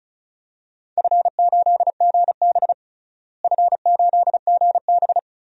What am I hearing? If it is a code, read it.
Morse code, heard as a keyed tone, about 36 wpm, 700 Hz. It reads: F8GB F8GB
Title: DXCC-100 Call Signs - 4 Characters - Repeated 3x - 35wpm